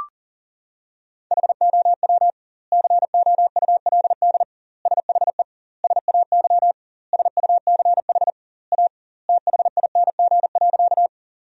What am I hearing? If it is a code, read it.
Morse code, heard as a keyed tone, about 40 wpm, 700 Hz. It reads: HOW COULD SHE SAY SUCH A THING.
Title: 2x Word-Spacing - Sentences from Top 200 Words - 40wpm